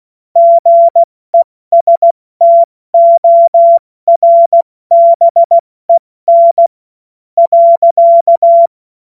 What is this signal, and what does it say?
Morse code, heard as a keyed tone, about 16 wpm, spaced as for 13 wpm, 680 Hz. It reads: GESTORBEN.
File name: Morsekurs / Witze / Witz 28. September 2021, 70 F80